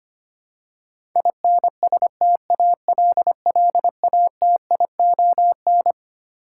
Morse code, heard as a keyed tone, about 25 wpm, 700 Hz. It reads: INSTALLATION
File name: Twelve Letter Words - 25wpm